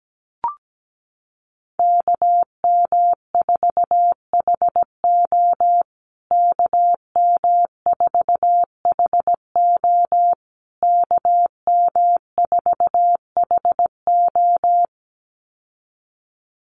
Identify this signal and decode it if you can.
Morse code, heard as a keyed tone, about 17 wpm, 700 Hz. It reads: KM4HO KM4HO KM4HO
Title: USA Call Signs Repeated 3x 17wpm